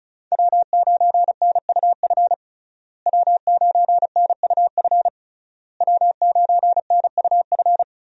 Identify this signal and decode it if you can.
Morse code, heard as a keyed tone, about 36 wpm, 700 Hz. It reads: W9NUF W9NUF W9NUF
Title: USA Call Signs - 5 Characters - Repeated 3x - 35wpm